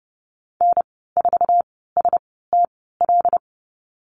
Morse code, heard as a keyed tone, about 30 wpm, spaced as for 17 wpm, 700 Hz. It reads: N4STL
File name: Lesson 14g - Number 2 - Call Signs 30wpm